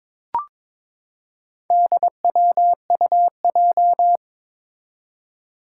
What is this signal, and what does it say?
Morse code, heard as a keyed tone, about 22 wpm, 700 Hz. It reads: DWUJ
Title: Four Challenging Letter-Number Combinations 22wpm